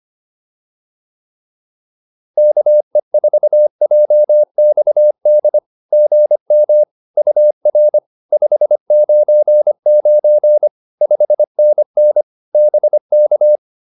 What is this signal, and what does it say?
Morse code, heard as a keyed tone, about 25 wpm, 600 Hz. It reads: KE4JXD GM UR 599 5NN BK